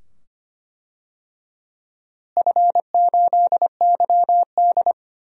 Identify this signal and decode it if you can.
Morse code, heard as a keyed tone, about 25 wpm, 700 Hz. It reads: F8YD
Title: DXCC-100 Call Signs - 4 Characters - Repeated 2x - 25wpm